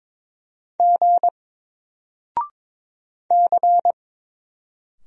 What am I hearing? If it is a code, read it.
Morse code, heard as a keyed tone, about 22 wpm, 700 Hz. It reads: G C
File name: Single-Letter-Number-22wpm